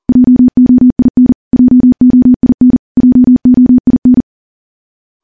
Morse code, heard as a keyed tone, about 40 wpm, 250 Hz. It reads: JOIN JOIN JOIN
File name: morse / morse-training-40WPM